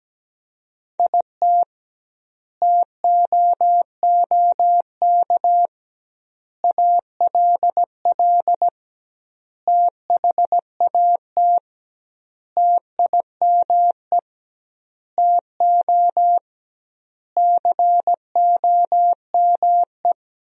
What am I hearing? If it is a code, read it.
Morse code, heard as a keyed tone, about 17 wpm, 700 Hz. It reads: IT TOOK ALL THAT TIME TO COME
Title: 2x Word-Spacing - Sentences from Top 300 Words - 17wpm